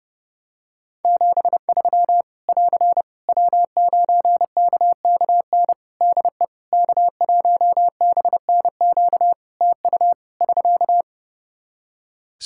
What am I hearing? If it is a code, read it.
Morse code, heard as a keyed tone, about 30 wpm, 700 Hz. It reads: 73 <AR> W9KKN DE K1BNQ TU <SK>
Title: K1USN SST QSO Practice 30wpm